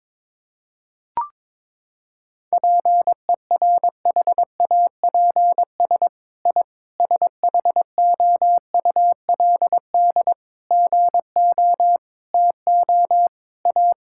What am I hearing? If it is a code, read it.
Morse code, heard as a keyed tone, about 22 wpm, 700 Hz. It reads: PERHAPS I SHOULD GO TO A
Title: Sentences from Top 1000 Words - Review Part 5 - 22wpm